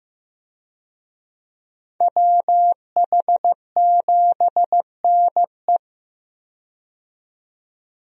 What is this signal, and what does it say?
Morse code, heard as a keyed tone, about 15 wpm, 700 Hz. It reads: WH7NE
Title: USA Call Signs - 5 Characters - 15wpm